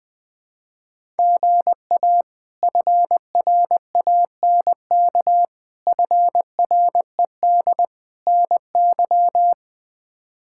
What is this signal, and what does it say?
Morse code, heard as a keyed tone, about 20 wpm, 700 Hz. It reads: GA FRANK FRED NY
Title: K1USN SST QSO Practice - 20wpm